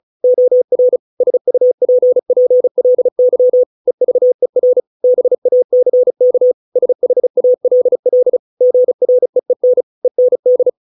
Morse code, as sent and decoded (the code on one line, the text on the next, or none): --- .-. / ... ..- .--. .--. .-.. -.-- / . ...- . .-. / -... .- -.-. -.- / ... .... .- .-.. .-.. / --. .-. . -. / . -. -..
OR SUPPLY EVER BACK SHALL GREEN END